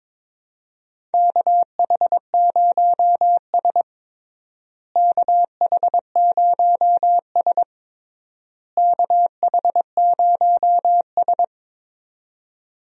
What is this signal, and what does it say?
Morse code, heard as a keyed tone, about 22 wpm, 700 Hz. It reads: KH0S KH0S KH0S
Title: USA Call Signs - 4 Characters - Repeated 3x - 22wpm